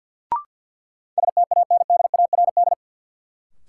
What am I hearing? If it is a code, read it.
Morse code, heard as a keyed tone, about 51 wpm, 700 Hz. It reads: STANDARD
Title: Sentences from Top 1200 Words - New Word Drill Alt - 50wpm